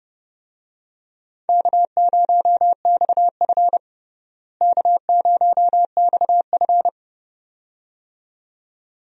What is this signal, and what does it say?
Morse code, heard as a keyed tone, about 31 wpm, 700 Hz. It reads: K0XF K0XF
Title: USA Call Signs - 4 Characters - Repeated 2x - 30wpm